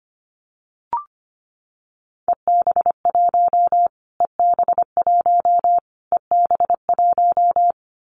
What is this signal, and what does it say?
Morse code, heard as a keyed tone, about 25 wpm, 700 Hz. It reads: EB1 EB1 EB1
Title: Callsign Prefixes for Top 25 DXCC - Repeated 3x - 25wpm